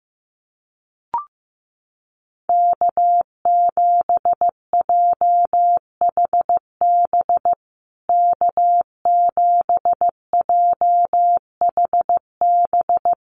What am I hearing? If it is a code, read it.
Morse code, heard as a keyed tone, about 15 wpm, 700 Hz. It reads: K7JHB K7JHB
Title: USA Call Signs Repeated 2x 15wpm